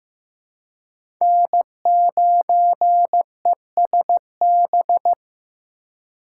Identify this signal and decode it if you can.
Morse code, heard as a keyed tone, about 15 wpm, 700 Hz. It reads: N9ESB
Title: USA Call Signs - 5 Characters - 4x Character Spacing - 15wpm